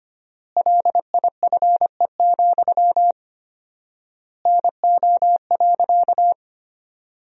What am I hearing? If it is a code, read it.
Morse code, heard as a keyed tone, about 25 wpm, 700 Hz. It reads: LIFE, NO.
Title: 4x Word-Spacing - Sentences from Top 300 Words - 25wpm